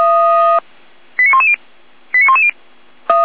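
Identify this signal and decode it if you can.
Five-tone selective call (ZVEI), heard as quick runs of five tones, tones of about 70 ms. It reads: 89110 89110
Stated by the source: ZVEI five-tone selective call 89071, received on SDR.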